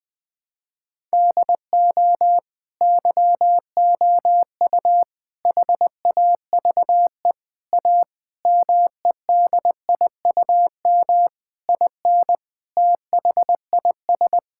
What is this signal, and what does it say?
Morse code, heard as a keyed tone, about 21 wpm, 700 Hz. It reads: DO YOU HAVE A MEDIUM IN THIS
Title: Sentences from Top 1000 Words - Review Part 6 - 20wpm